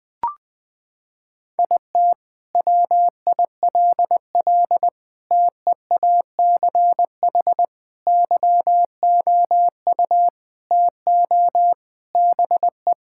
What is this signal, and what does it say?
Morse code, heard as a keyed tone, about 20 wpm, 700 Hz. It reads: IT WILL TEACH YOU TO BE
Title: Sentences from Top 1600 Words 20wpm